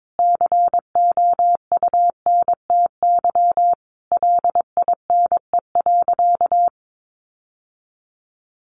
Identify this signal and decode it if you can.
Morse code, heard as a keyed tone, about 22 wpm, 700 Hz. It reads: COUNTY LINE.
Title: Sentences from Top 1200 Words 22wpm